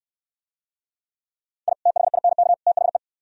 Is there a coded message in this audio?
Yes: Morse code from 1.7 to 3.0 s, heard as a keyed tone, about 69 wpm, 700 Hz. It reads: I THINK THE